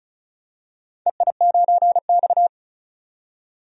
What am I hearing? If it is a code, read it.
Morse code, heard as a keyed tone, about 36 wpm, 700 Hz. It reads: EI9X
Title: DXCC-100 Call Signs - 4 Characters - 35wpm